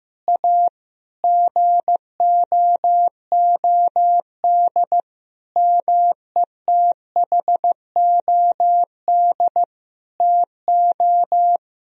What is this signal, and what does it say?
Morse code, heard as a keyed tone, about 15 wpm, 700 Hz. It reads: A GOOD METHOD TO
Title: Sentences from Top 1000 Words - Review Part 3 - 15wpm